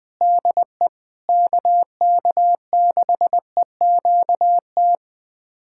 Morse code, heard as a keyed tone, about 21 wpm, 700 Hz. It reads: DE KK6EQT
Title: USA Callsign DE Callsign - 6 Characters - 20wpm